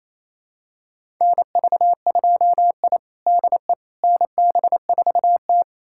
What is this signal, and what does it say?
Morse code, heard as a keyed tone, about 28 wpm, 700 Hz. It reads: NV2I DE NB4T
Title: USA Callsign DE Callsign - 4 Characters - 28wpm